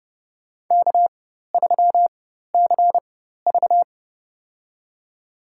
Morse code, heard as a keyed tone, about 32 wpm, spaced as for 14 wpm, 700 Hz. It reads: K3CV
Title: USA Call Signs - 4 Characters - 4x Character Spacing - 30wpm